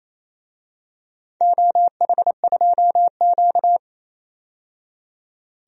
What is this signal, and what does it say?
Morse code, heard as a keyed tone, about 28 wpm, 700 Hz. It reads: OH2Q
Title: DXCC-100 Call Signs - 4 Characters - 28wpm